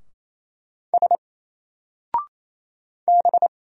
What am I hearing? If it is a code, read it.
Morse code, heard as a keyed tone, about 28 wpm, 700 Hz. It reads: S B
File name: Single-Letter-Number-28wpm